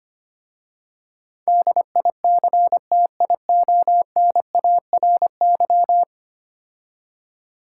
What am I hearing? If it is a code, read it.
Morse code, heard as a keyed tone, about 25 wpm, 700 Hz. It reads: DICTIONARY
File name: 10 Letter Words - First 3 Letters Given - 25wpm